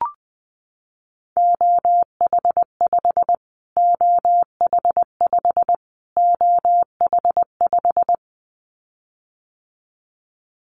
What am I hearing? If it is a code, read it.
Morse code, heard as a keyed tone, about 19 wpm, 700 Hz. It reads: OH5 OH5 OH5
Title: Callsign Prefixes for Top 25 DXCC - Repeated 3x - 20wpm